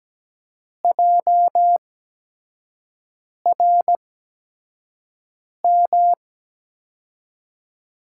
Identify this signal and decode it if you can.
Morse code, heard as a keyed tone, about 17 wpm, spaced as for 5 wpm, 700 Hz. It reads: JRM
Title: USA Call Signs - 5 Characters - 8x Character Spacing - 17wpm